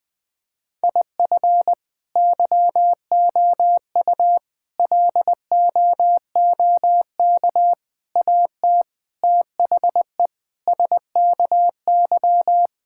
Morse code, heard as a keyed tone, about 20 wpm, 700 Hz. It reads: IF YOU LOOK AT THE SKY
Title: Sentences with 4-Word Phrases - 20wpm